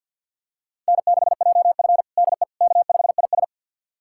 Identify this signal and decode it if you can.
Morse code, heard as a keyed tone, about 52 wpm, 700 Hz. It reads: N6JF DE K5IS
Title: USA Callsign DE Callsign - 4 Characters - 50wpm